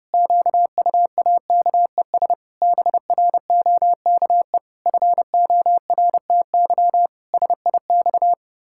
Morse code, heard as a keyed tone, about 28 wpm, 700 Hz. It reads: QUAKES BROKE FORTY SIX
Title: Pangram Challenge v2 - 30wpm